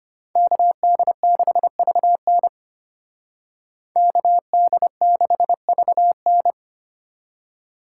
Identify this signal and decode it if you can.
Morse code, heard as a keyed tone, about 32 wpm, 700 Hz. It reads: KD6VN KD6VN